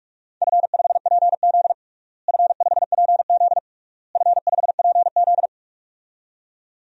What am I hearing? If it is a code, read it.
Morse code, heard as a keyed tone, about 43 wpm, 700 Hz. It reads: F5PZ F5PZ F5PZ
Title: DXCC-100 Call Signs - 4 Characters - Repeated 3x - 45wpm